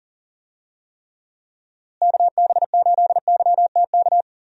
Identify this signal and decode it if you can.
Morse code, heard as a keyed tone, about 39 wpm, 700 Hz. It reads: KB8YTK
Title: USA Call Signs - 6 Characters - Repeated 2x - 40wpm